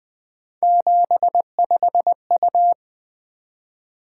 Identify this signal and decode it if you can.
Morse code, heard as a keyed tone, about 20 wpm, 700 Hz. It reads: 75U
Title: Three Letter-Number Combinations - 20wpm